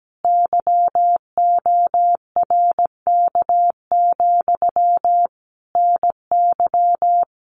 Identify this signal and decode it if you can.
Morse code, heard as a keyed tone, about 17 wpm, 700 Hz. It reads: YORK, NY
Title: Common QSO Phrases Encore 17wpm